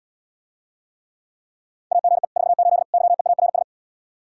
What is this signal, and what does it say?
Morse code, heard as a keyed tone, about 80 wpm, 700 Hz. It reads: AGE 48 YEARS